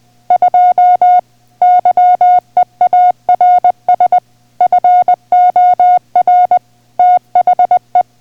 Morse code, heard as a keyed tone, about 20 wpm, 720 Hz. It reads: 2 YEARS FOR THE